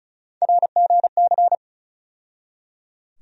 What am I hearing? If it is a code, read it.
Morse code, heard as a keyed tone, about 36 wpm, 700 Hz. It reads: RGC
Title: USA Call Sign Postfixes - 3 Letters 35wpm